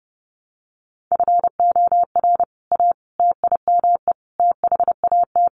Morse code, heard as a keyed tone, about 30 wpm, 700 Hz. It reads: FOR A TIME THAT
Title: Sentences from Top 700 Words 30wpm